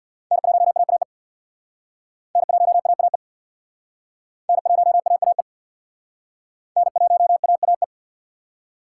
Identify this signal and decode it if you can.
Morse code, heard as a keyed tone, about 61 wpm, 700 Hz. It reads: N1AAE N1AAE N1AAE N1AAE
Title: Speed Racing - USA 1x3 Call Signs 50wpm